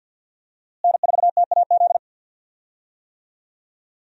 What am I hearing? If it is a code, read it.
Morse code, heard as a keyed tone, about 48 wpm, 700 Hz. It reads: N4TAZ
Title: USA Call Signs - 5 Characters - Repeated 2x - 50wpm